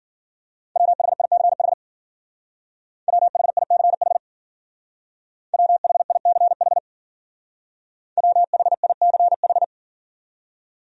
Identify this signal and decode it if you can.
Morse code, heard as a keyed tone, about 50 wpm, 700 Hz. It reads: WHICH WHICH WHICH WHICH